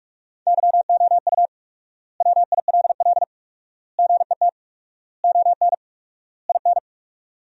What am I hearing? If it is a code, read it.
Morse code, heard as a keyed tone, about 42 wpm, 700 Hz. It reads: YOU WILL GET ON IN
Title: 4x Word-Spacing - Sentences from Top 200 Words - 45wpm